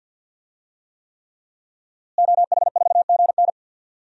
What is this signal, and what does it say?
Morse code, heard as a keyed tone, about 52 wpm, 700 Hz. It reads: OH4GN
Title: DXCC-100 Call Signs - 5 Characters - Repeated 3x - 50wpm